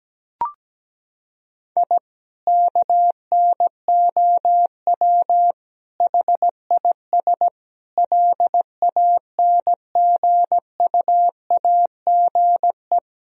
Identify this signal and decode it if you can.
Morse code, heard as a keyed tone, about 17 wpm, 700 Hz. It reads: I KNOW HIS LANGUAGE
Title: Sentences from Top 1000 Words - Review Part 3 - 17wpm